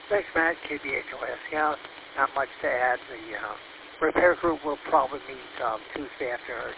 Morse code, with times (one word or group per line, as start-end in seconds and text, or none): none